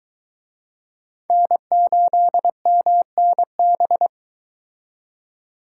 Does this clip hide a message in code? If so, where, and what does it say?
Yes: Morse code from 1.3 to 4.1 s, heard as a keyed tone, about 23 wpm, 700 Hz. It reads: N8MNB